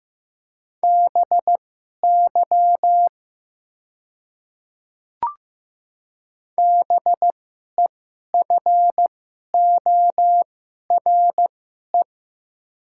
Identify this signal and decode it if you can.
Morse code, heard as a keyed tone, about 15 wpm, spaced as for 11 wpm, 700 Hz. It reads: BY BEFORE
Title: 2x Character Spacing - Top 300 Words - 15wpm